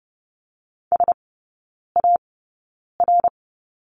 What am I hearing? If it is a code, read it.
Morse code, heard as a keyed tone, about 30 wpm, spaced as for 9 wpm, 700 Hz. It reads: SAR
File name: 081 Lesson 14e - Number 2 - Three Characters 30wpm